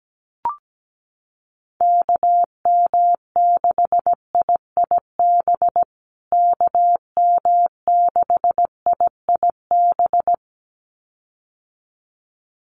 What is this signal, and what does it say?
Morse code, heard as a keyed tone, about 17 wpm, 700 Hz. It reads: KM6IIB KM6IIB
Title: USA Call Signs Repeated 2x 17wpm